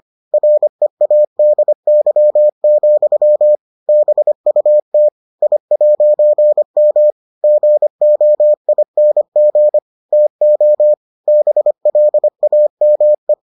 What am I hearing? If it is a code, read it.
Morse code, heard as a keyed tone, about 25 wpm, 600 Hz. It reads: READY, BUT I'M GOING TO BLAME